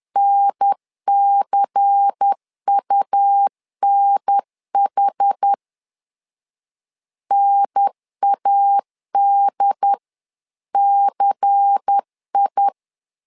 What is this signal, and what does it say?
Morse code, heard as a keyed tone, about 10 wpm, 800 Hz. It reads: NCUNH NADCI
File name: LEZIONI / 5 LBVC / 5 LBVC 10wpm 2349